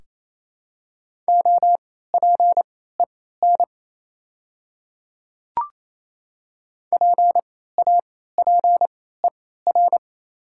Morse code, heard as a keyed tone, about 28 wpm, spaced as for 16 wpm, 700 Hz. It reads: OPEN PAPER
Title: 3x Character Spacing - Top 500 Words - 28wpm